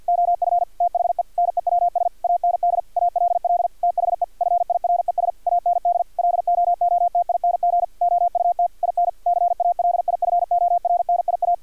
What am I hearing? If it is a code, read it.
Morse code, heard as a keyed tone, about 50 wpm, 700 Hz, with radio noise. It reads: OF THE NEWS AND ALL THE FIRES AND LOOTING OUT IN CALIFORNIA
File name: archived / CRSnet